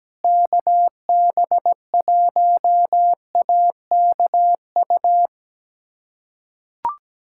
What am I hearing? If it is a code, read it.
Morse code, heard as a keyed tone, about 17 wpm, 700 Hz. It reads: KB1AKU
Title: USA Callsign DE Callsign 17wpm